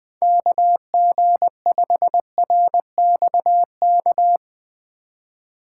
Morse code, heard as a keyed tone, about 20 wpm, 700 Hz. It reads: KG5RXK
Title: USA Call Signs - 6 Characters - 3x Character Spacing - 20wpm